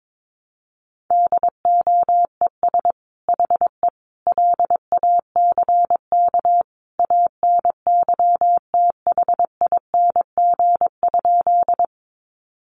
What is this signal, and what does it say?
Morse code, heard as a keyed tone, about 22 wpm, 700 Hz. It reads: DOES HE LACK ANYTHING?
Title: Sentences from Top 1100 Words 22wpm